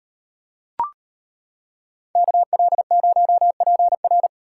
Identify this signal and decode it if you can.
Morse code, heard as a keyed tone, about 33 wpm, 700 Hz. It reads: KL0PR